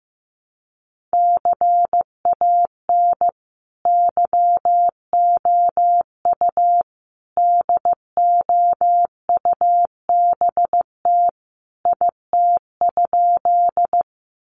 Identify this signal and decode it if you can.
Morse code, heard as a keyed tone, about 15 wpm, 700 Hz. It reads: CAN YOU DOUBT IT?
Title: Sentences from Top 1500 Words 15wpm